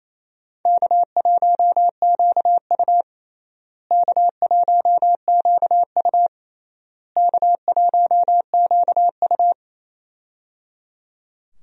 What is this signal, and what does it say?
Morse code, heard as a keyed tone, about 28 wpm, 700 Hz. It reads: K1QU K1QU K1QU
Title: USA Call Signs - 4 Characters - Repeated 3x - 28wpm